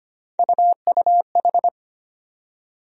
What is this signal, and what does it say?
Morse code, heard as a keyed tone, about 25 wpm, 700 Hz. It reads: UUH
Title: USA Call Sign Postfixes - 3 Letters 25wpm